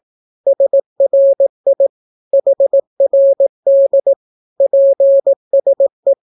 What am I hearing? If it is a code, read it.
Morse code, heard as a keyed tone, about 18 wpm, 550 Hz. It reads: SRI HRD PSE
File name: naukapodstaw / skroty / skroty